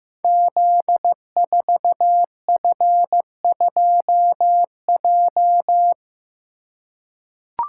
Morse code, heard as a keyed tone, about 15 wpm, 700 Hz. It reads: Z4F2J
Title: Code Groups - 5 Letter:Numbers - 15wpm